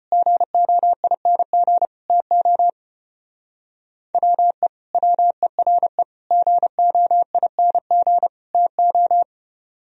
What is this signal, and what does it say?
Morse code, heard as a keyed tone, about 33 wpm, 700 Hz. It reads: GOING TO WE WERE GOING TO